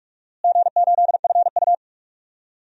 Morse code, heard as a keyed tone, about 43 wpm, 700 Hz. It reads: G8FU
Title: DXCC-100 Call Signs - 4 Characters - Repeated 3x - 45wpm